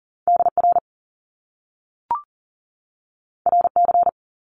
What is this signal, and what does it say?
Morse code, heard as a keyed tone, about 39 wpm, 700 Hz. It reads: DR RC
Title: Top Two Letter Combinations in Words 40wpm